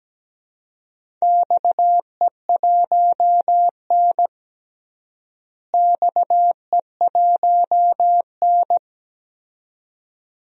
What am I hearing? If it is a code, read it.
Morse code, heard as a keyed tone, about 17 wpm, 700 Hz. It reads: XE1N XE1N
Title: DXCC-100 Call Signs - 4 Characters - Repeated 2x - 17wpm